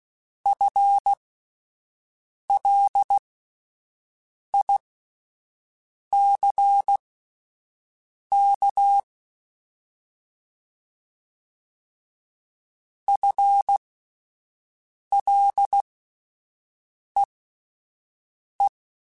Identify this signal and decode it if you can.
Morse code, heard as a keyed tone, about 16 wpm, spaced as for 5 wpm, 800 Hz. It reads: FLICK FLEE
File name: Alphabetical / Alphabetical mp3s / Fs @5